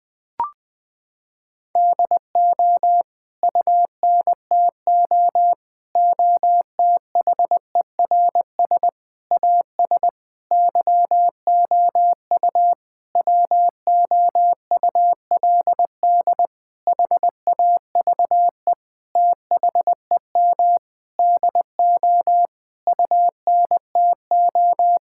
Morse code, heard as a keyed tone, about 21 wpm, 700 Hz. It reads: DO UNTO OTHERS AS YOU WOULD HAVE THEM DO UNTO
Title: Sentences with 9-Word Phrases - 20wpm